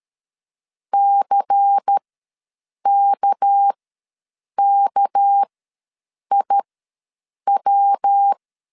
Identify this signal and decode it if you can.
Morse code, heard as a keyed tone, about 13 wpm, spaced as for 7 wpm, 800 Hz. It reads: CKKIW
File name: LEZIONI / 6 PKX0 / 6 PK ws 12wpm 3116